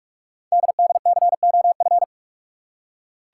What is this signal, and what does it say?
Morse code, heard as a keyed tone, about 44 wpm, 700 Hz. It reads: DDCOF
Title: Code Groups - 5 Letters - 45wpm